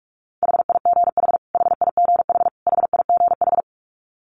Morse code, heard as a keyed tone, about 46 wpm, 700 Hz. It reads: HIGH HIGH HIGH
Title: Top 300 Words Repeated 45wpm